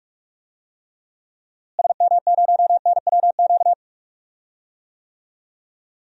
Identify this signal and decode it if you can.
Morse code, heard as a keyed tone, about 41 wpm, 700 Hz. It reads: SM0NWQ
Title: DXCC-100 Call Signs - 6 Characters - Repeated 3x - 45wpm